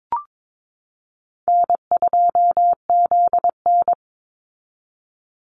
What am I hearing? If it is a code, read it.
Morse code, heard as a keyed tone, about 22 wpm, 700 Hz. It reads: N2ZN
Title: Call Signs Repeated 2x - 22wpm